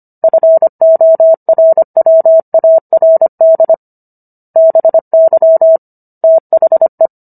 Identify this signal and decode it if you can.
Morse code, heard as a keyed tone, about 25 wpm, 650 Hz. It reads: FORWARD BY THE